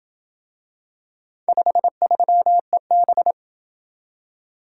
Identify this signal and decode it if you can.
Morse code, heard as a keyed tone, about 27 wpm, 700 Hz. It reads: 53EB